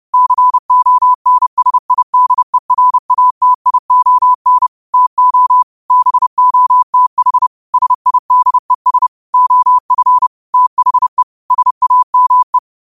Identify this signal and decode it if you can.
Morse code, heard as a keyed tone, about 30 wpm, 1000 Hz. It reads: CONSIDERATION TO BOTH SIDES OF THE SAME